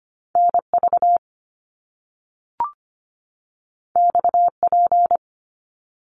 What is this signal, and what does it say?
Morse code, heard as a keyed tone, about 25 wpm, 700 Hz. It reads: NV XP
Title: Top Two Letter Combinations in Words 25wpm